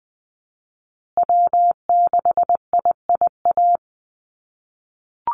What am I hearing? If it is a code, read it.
Morse code, heard as a keyed tone, about 20 wpm, 700 Hz. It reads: W6IIA
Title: USA Call Signs Repeated 2x 20wpm